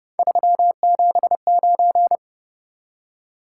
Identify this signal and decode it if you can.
Morse code, heard as a keyed tone, about 28 wpm, 700 Hz. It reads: 379